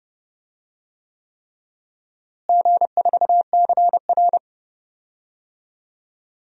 Morse code, heard as a keyed tone, about 30 wpm, 700 Hz. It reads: G4CR
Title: DXCC-100 Call Signs - 4 Characters - Repeated 3x - 30wpm